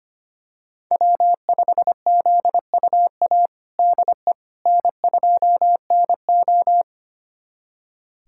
Morse code, heard as a keyed tone, about 25 wpm, 700 Hz. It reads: W5ZUA DE N2NO
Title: USA Callsign DE Callsign 25wpm